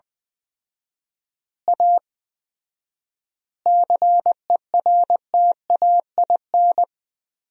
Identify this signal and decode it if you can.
Morse code, heard as a keyed tone, about 20 wpm, 700 Hz. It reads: A CERTAIN
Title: 4x Word-Spacing - Sentences from Top 500 Words - 20wpm